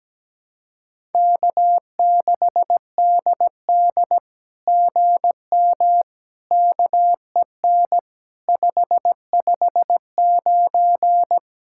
Morse code, heard as a keyed tone, about 17 wpm, 700 Hz. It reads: K6DD GM KEN 559
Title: SOTA QSO Practice 17wpm